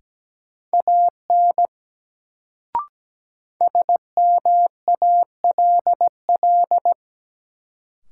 Morse code, heard as a keyed tone, about 17 wpm, 700 Hz. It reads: AN SMALL
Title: Sets of 2 Words - New Word Drill - 17wpm